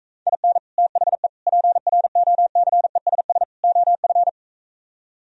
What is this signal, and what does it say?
Morse code, heard as a keyed tone, about 45 wpm, 690 Hz. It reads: IN THE PROCESS OF